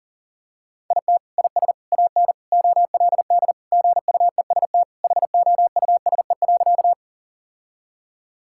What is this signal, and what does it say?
Morse code, heard as a keyed tone, about 40 wpm, 700 Hz. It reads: IT IS AN OLD GUEST HOUSE.
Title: Sentences from Top 1600 Words 40wpm